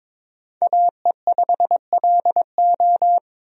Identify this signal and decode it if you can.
Morse code, heard as a keyed tone, about 22 wpm, 700 Hz. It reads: AE5LO
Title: USA Call Signs - 5 Characters - 22wpm